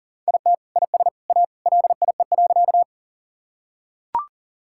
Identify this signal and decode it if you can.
Morse code, heard as a keyed tone, about 40 wpm, 700 Hz. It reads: IT IS A LIE.
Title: Sentences from Top 1600 Words 40wpm